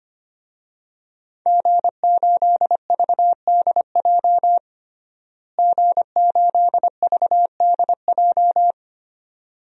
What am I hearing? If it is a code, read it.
Morse code, heard as a keyed tone, about 25 wpm, 700 Hz. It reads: G8VDJ G8VDJ
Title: DXCC-100 Call Signs - 5 Characters - Repeated 2x - 25wpm